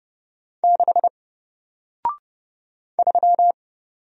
Morse code, heard as a keyed tone, about 30 wpm, 700 Hz. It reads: 6 3
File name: Single-Letter-Number-30wpm